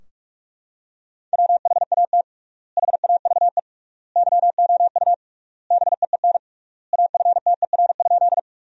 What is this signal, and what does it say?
Morse code, heard as a keyed tone, about 46 wpm, 700 Hz. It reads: WHAT HAVE YOU BEEN AFTER?
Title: 3x Word-Spacing - Sentences from Top 200 Words - 45wpm